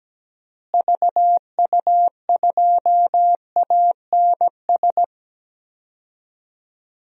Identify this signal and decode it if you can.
Morse code, heard as a keyed tone, about 17 wpm, 700 Hz. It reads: VU2ANS
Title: DXCC-100 Call Signs - 6 Characters - Repeated 3x - 17wpm